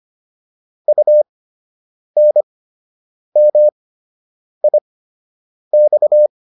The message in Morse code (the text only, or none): UNMIX